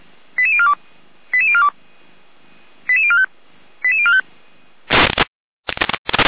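Five-tone selective call (ZVEI), heard as quick runs of five tones, tones of about 70 ms. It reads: 88042 88042 88045 88045
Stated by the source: Five-tone ZVEI selective call received by SDR, from the feuerwehr-orlamuende.org (F85360) audio server.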